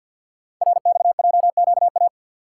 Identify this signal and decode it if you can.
Morse code, heard as a keyed tone, about 49 wpm, 700 Hz. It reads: RXJQA
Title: Code Groups - 5 Letters - 50wpm